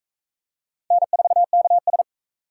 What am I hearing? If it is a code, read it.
Morse code, heard as a keyed tone, about 38 wpm, 700 Hz. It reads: N4KS